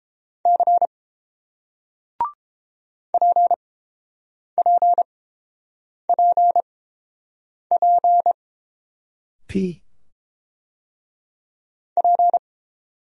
Morse code, heard as a keyed tone, about 29 wpm, spaced as for 7 wpm, 700 Hz. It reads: C PPPP P